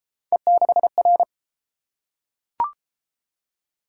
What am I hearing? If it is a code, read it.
Morse code, heard as a keyed tone, about 26 wpm, 700 Hz. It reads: E6R
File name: Speed-Racing - Three Letter-Number Combinations - 22wpm